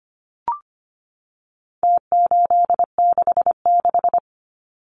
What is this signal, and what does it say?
Morse code, heard as a keyed tone, about 25 wpm, 700 Hz. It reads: T866
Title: Numbers - Contest Style - 4-Digit 25wpm